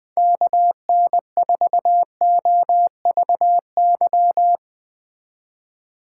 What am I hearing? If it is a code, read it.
Morse code, heard as a keyed tone, about 21 wpm, 700 Hz. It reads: KN4OVY
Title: Speed Racing - USA 2x3 Call Signs 20wpm